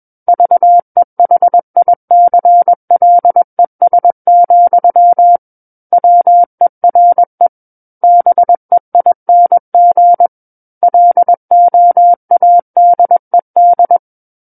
Morse code, heard as a keyed tone, about 21 wpm, 700 Hz. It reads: VEHICLES, WERE BEING LOADED
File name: Book One / 12-Chapter 12 - 21 WPM